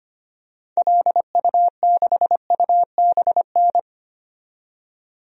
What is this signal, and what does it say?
Morse code, heard as a keyed tone, about 25 wpm, 700 Hz. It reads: LU6UBN
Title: DXCC-100 Call Signs - 6 Characters - 25wpm